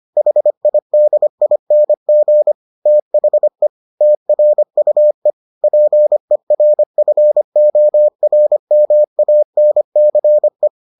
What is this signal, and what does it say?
Morse code, heard as a keyed tone, about 25 wpm, 600 Hz. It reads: HIDING THE TRUE PERFORMANCE